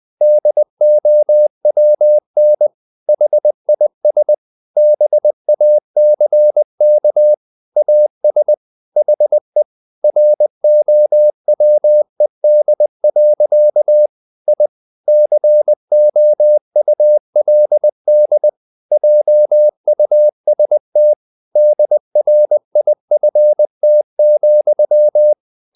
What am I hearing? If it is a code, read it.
Morse code, heard as a keyed tone, about 20 wpm, 600 Hz. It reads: DOWN HIS BACK AS HE ROWED. I COULD JUST DRIFT,
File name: the old man and the sea / the old man and the sea part 1 20wpm